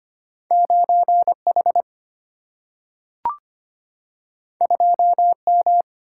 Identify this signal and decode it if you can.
Morse code, heard as a keyed tone, about 25 wpm, 700 Hz. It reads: 9H 2M
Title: Single-Letter Single-Number 25wpm